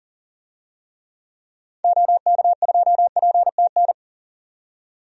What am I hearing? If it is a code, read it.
Morse code, heard as a keyed tone, about 40 wpm, 700 Hz. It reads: OK2PTN
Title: DXCC-100 Call Signs - 6 Characters - 40wpm